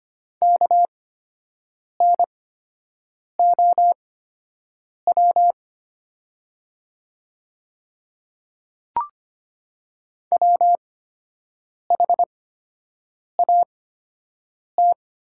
Morse code, heard as a keyed tone, about 25 wpm, spaced as for 7 wpm, 700 Hz. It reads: KNOW WHAT
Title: 8x Character Spacing - Top 100 Words - 25wpm